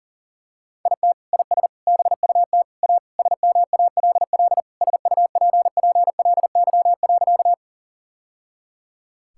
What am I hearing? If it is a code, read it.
Morse code, heard as a keyed tone, about 40 wpm, 700 Hz. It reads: IT IS BUT A SMALL SUPPLY.
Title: Sentences from Top 1000 Words - Review Part 5 - 40wpm